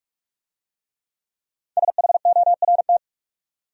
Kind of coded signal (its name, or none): Morse code